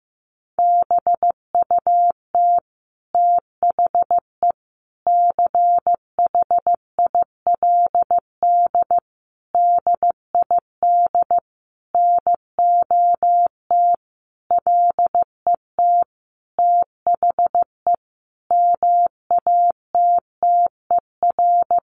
Morse code, heard as a keyed tone, about 15 wpm, 700 Hz. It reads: BUT THE CHILD DID NOT LET THE MATTER
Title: Sentences from Top 1400 Words 15wpm